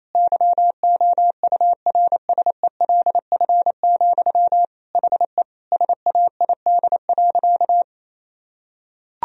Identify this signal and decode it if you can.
Morse code, heard as a keyed tone, about 28 wpm, 700 Hz. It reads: YOURSELF, HE SAID.
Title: Sentences from Top 1000 Words - Review Part 3 - 28wpm